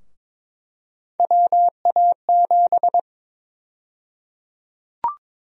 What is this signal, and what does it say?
Morse code, heard as a keyed tone, about 21 wpm, 700 Hz. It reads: WA7
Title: USA Call Sign Prefixes - 22wpm